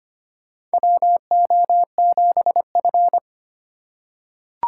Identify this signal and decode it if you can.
Morse code, heard as a keyed tone, about 25 wpm, 700 Hz. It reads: WO7F
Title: USA Call Signs - 4 Characters - 4x Character Spacing - 25wpm